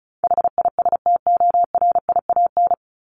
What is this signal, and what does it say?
Morse code, heard as a keyed tone, about 34 wpm, 700 Hz. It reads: HISTORIAN
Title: Top 1751-2000 Words 35wpm